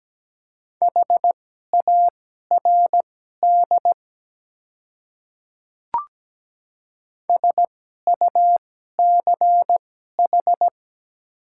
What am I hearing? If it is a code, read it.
Morse code, heard as a keyed tone, about 17 wpm, spaced as for 12 wpm, 700 Hz. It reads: HARD SUCH
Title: 2x Character Spacing - Top 300 Words - 17wpm